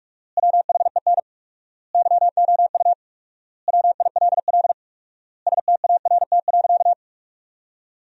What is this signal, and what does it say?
Morse code, heard as a keyed tone, about 41 wpm, 700 Hz. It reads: WHEN YOU WILL START.
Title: 4x Word-Spacing - Sentences from Top 300 Words - 45wpm